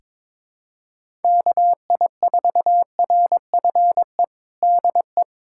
Morse code, heard as a keyed tone, about 22 wpm, 700 Hz. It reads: KI4RFE DE
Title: USA Callsign DE Callsign - 6 Characters - 22wpm